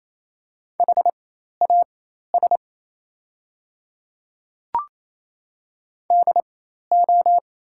Morse code, heard as a keyed tone, about 28 wpm, spaced as for 13 wpm, 700 Hz. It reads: HAS DO
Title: 4x Character Spacing - Top 100 Words - 28wpm